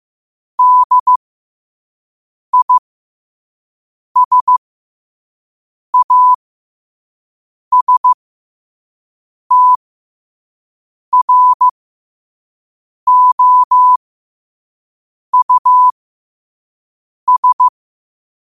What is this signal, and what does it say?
Morse code, heard as a keyed tone, about 15 wpm, spaced as for 5 wpm, 1000 Hz. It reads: DISASTROUS
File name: 5wpm